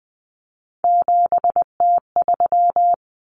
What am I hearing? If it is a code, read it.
Morse code, heard as a keyed tone, about 20 wpm, 700 Hz. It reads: 7T3
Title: Common Cut Numbers - 3-Digit 20wpm